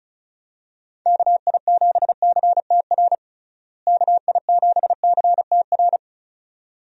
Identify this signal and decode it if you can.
Morse code, heard as a keyed tone, about 36 wpm, 700 Hz. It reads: KI7CTR KI7CTR
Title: USA Call Signs - 6 Characters - Repeated 2x - 35wpm